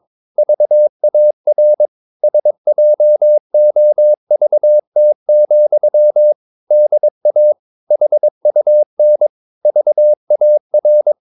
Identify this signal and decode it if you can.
Morse code, heard as a keyed tone, about 22 wpm, 600 Hz. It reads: VAR SJOVT, DA HUN VAR
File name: Ferie QRS 21wpm